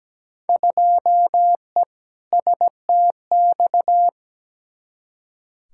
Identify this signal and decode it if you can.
Morse code, heard as a keyed tone, about 17 wpm, 700 Hz. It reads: 2E STX
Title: Field Day Exchanges Encore - 17wpm